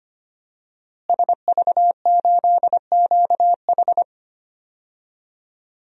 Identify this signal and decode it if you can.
Morse code, heard as a keyed tone, about 25 wpm, 700 Hz. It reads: SV8QH
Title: DXCC-100 Call Signs - 5 Characters - Repeated 2x - 25wpm